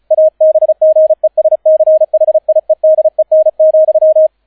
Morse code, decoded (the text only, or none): ABGESCHIEDEN,